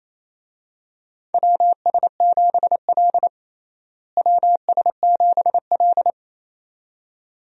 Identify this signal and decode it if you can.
Morse code, heard as a keyed tone, about 27 wpm, 700 Hz. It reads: WS7L WS7L
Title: USA Call Signs - 4 Characters - Repeated 2x - 28wpm